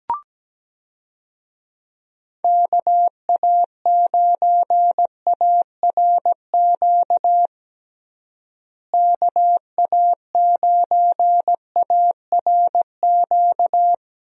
Morse code, heard as a keyed tone, about 17 wpm, 700 Hz. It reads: KA9ARQ KA9ARQ
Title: USA Call Signs - 6 Characters - Repeated 2x - 17wpm